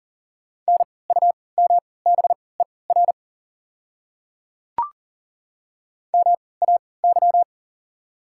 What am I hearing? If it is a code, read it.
Morse code, heard as a keyed tone, about 40 wpm, spaced as for 23 wpm, 690 Hz. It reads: NUMBER MAY
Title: 3x Character Spacing - Top 100 Words - 40wpm